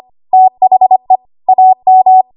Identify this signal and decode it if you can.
Morse code, heard as a keyed tone, about 25 wpm, 750 Hz. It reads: THE AM
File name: Archive / 25 WPM / 25WPM